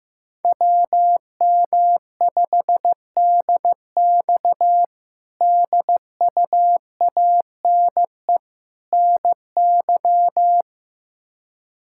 Morse code, heard as a keyed tone, about 15 wpm, 700 Hz. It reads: WM5DX DUANE NY
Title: K1USN SST QSO Practice - 15wpm